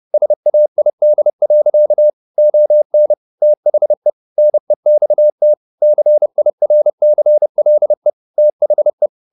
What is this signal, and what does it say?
Morse code, heard as a keyed tone, about 30 wpm, 600 Hz. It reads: SAID. ON THE NEXT CIRCLE THE